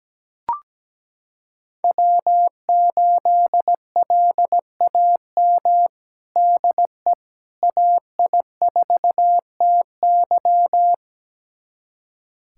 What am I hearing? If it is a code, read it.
Morse code, heard as a keyed tone, about 17 wpm, 700 Hz. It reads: W8LAM DE AI4TY
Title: USA Callsign DE Callsign - 5 Characters - 17wpm